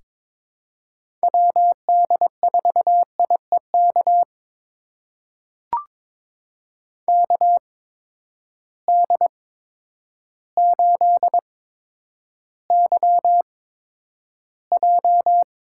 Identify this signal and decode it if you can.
Morse code, heard as a keyed tone, about 21 wpm, spaced as for 10 wpm, 700 Hz. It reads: WD4IEK KD8YJ
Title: USA Call Signs - 6 Characters - 8x Character Spacing - 22wpm